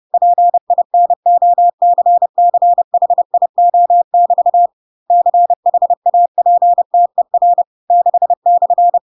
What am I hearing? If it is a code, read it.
Morse code, heard as a keyed tone, about 30 wpm, 700 Hz. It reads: PINOCCHIO<BT> CHAPTER 6/
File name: Pinocchio / Pinocchio ch6 30 30 ik1hge